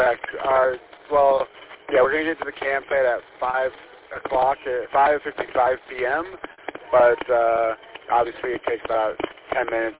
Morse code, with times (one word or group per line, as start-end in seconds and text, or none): none